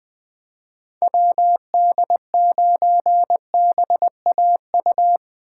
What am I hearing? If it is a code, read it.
Morse code, heard as a keyed tone, about 20 wpm, 700 Hz. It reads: WD9BAU
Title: USA Call Signs - 6 Characters - 2x Character Spacing - 20wpm